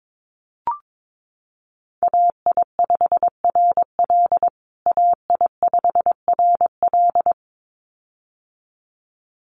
Morse code, heard as a keyed tone, about 22 wpm, 700 Hz. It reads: AI5RL AI5RL
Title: USA Call Signs Repeated 2x 22wpm